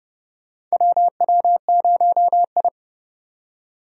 Morse code, heard as a keyed tone, about 29 wpm, 700 Hz. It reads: WW0I
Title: USA Call Signs - 4 Characters - 30wpm